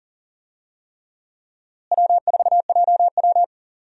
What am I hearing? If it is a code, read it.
Morse code, heard as a keyed tone, about 40 wpm, 700 Hz. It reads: W4JW
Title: USA Call Signs - 4 Characters - Repeated 3x - 40wpm